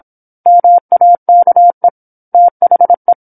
Morse code, heard as a keyed tone, about 26 wpm, 700 Hz. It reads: MAKE THE
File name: 08-Chapter-08 26 WPM